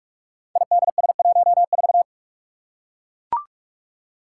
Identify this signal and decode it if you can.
Morse code, heard as a keyed tone, about 44 wpm, 700 Hz. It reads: IDS14
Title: Five Challenging Letter-Number Combinations 45wpm